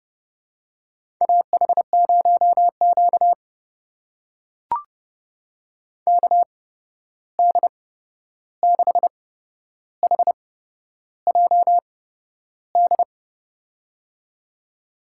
Morse code, heard as a keyed tone, about 32 wpm, spaced as for 11 wpm, 700 Hz. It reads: AH0Q KD6HJD
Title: USA Call Signs - 8x Character Spacing - 30wpm